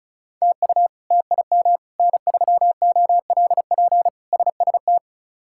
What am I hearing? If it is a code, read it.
Morse code, heard as a keyed tone, about 36 wpm, 700 Hz. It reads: TU TIM N3OLP SST
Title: K1USN SST QSO Practice - 35wpm